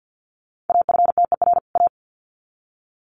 Morse code, heard as a keyed tone, about 50 wpm, 700 Hz. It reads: AFTER A
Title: Sets of 2 Words 50wpm